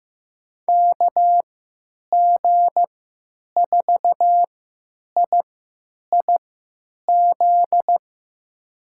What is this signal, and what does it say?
Morse code, heard as a keyed tone, about 15 wpm, spaced as for 9 wpm, 700 Hz. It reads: KG4IIZ
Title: USA Call Signs - 6 Characters - 3x Character Spacing - 15wpm